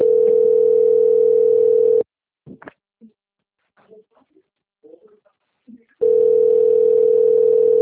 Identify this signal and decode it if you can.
Ringback tone (North America), 440 + 480 Hz.
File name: out- - - - - .3630